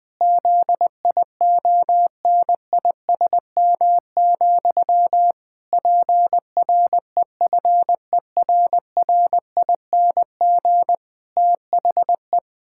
Morse code, heard as a keyed tone, about 20 wpm, 700 Hz. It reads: ZIONISM, PREFERRING THE
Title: Pangram Challenge v2 - 20wpm